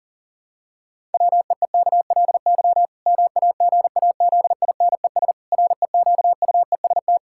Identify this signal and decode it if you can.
Morse code, heard as a keyed tone, about 37 wpm, 700 Hz. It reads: WEEKLY MAGAZINES REQUEST
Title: Pangram Challenge v2 - 40wpm